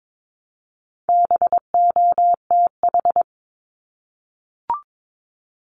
Morse code, heard as a keyed tone, about 22 wpm, 700 Hz. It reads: BOTH
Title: Top 500 Words Repeated 22wpm